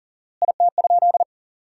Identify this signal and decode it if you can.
Morse code, heard as a keyed tone, about 40 wpm, 700 Hz. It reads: IT?
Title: 4x Word-Spacing - Sentences from Top 400 Words - 40wpm